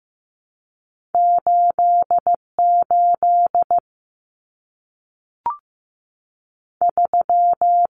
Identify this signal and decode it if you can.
Morse code, heard as a keyed tone, about 15 wpm, 700 Hz. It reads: 8 8 3
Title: Common Cut Numbers - 2-Digit 15wpm